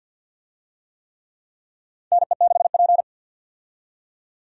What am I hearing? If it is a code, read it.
Morse code, heard as a keyed tone, about 46 wpm, 700 Hz. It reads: NE6P
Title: USA Call Signs - 4 Characters - 50wpm